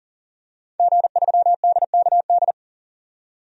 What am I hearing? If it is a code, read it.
Morse code, heard as a keyed tone, about 40 wpm, 700 Hz. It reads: G3DKD
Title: DXCC-100 Call Signs - 5 Characters - 40wpm